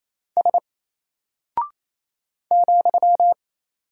Morse code, heard as a keyed tone, about 27 wpm, 700 Hz. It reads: S ,
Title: Single Character (Letter:Number:Punctuation:Prosign) 28wpm